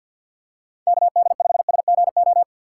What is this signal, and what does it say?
Morse code, heard as a keyed tone, about 51 wpm, 700 Hz. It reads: KD5SGO
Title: USA Call Signs - 6 Characters - 4x Character Spacing - 50wpm